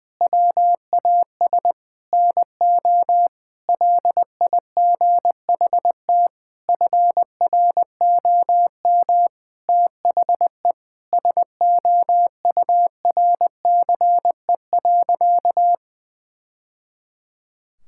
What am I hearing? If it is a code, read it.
Morse code, heard as a keyed tone, about 20 wpm, 700 Hz. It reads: WAS NO LIGHT FROM THE SOURCE.
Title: Sentences from Top 1000 Words - Review Part 5 - 20wpm